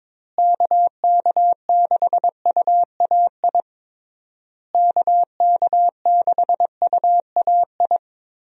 Morse code, heard as a keyed tone, about 22 wpm, 700 Hz. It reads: KK6UAI KK6UAI
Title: USA Call Signs - 6 Characters - Repeated 3x - 22wpm